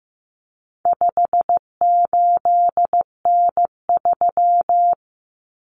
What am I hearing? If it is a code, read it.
Morse code, heard as a keyed tone, about 15 wpm, 700 Hz. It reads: 58N3
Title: Common Cut Numbers - 4-Digit 15wpm